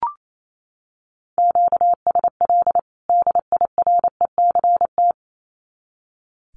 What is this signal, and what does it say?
Morse code, heard as a keyed tone, about 28 wpm, 700 Hz. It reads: QSL DIRECT
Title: Common QSO Phrases 28wpm